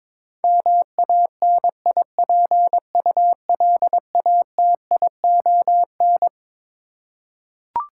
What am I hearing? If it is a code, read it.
Morse code, heard as a keyed tone, about 22 wpm, 700 Hz. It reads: MANIPULATION
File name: Twelve Letter Words - 22wpm